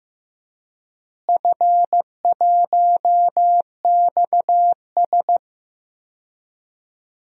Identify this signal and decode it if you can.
Morse code, heard as a keyed tone, about 15 wpm, 700 Hz. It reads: F1XS
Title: DXCC-100 Call Signs - 4 Characters - Repeated 2x - 15wpm